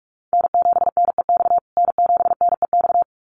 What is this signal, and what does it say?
Morse code, heard as a keyed tone, about 44 wpm, 700 Hz. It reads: N7NEX N7NEX
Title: USA Call Signs Repeated 2x 45wpm